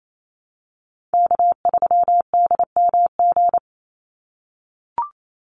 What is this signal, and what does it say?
Morse code, heard as a keyed tone, about 28 wpm, 700 Hz. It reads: K3DMG
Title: Call Signs - 28wpm